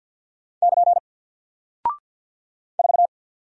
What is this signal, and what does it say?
Morse code, heard as a keyed tone, about 48 wpm, 700 Hz. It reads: <KN> 4
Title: Single Character (Letter:Number:Punctuation:Prosign) Alt 50wpm